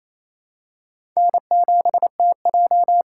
Morse code, heard as a keyed tone, about 28 wpm, 700 Hz. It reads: N7TJ
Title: USA Call Signs - 3x Character Spacing - 28wpm